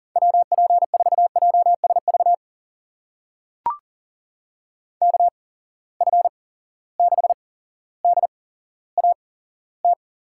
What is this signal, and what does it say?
Morse code, heard as a keyed tone, about 40 wpm, spaced as for 17 wpm, 700 Hz. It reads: WP4JSV KF6DAT
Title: USA Call Signs - 6 Characters - 8x Character Spacing - 40wpm